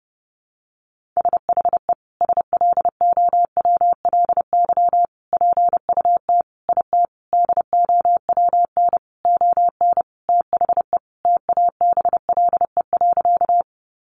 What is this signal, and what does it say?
Morse code, heard as a keyed tone, about 30 wpm, 700 Hz. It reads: SHE SLOWLY PUT IT DOWN ON THE TABLE.
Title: Sentences from Top 1100 Words 30wpm